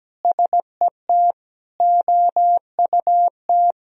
Morse code, heard as a keyed tone, about 17 wpm, 700 Hz. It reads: SET OUT
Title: Sentences from Top 1000 Words - Review Part 1 - 17wpm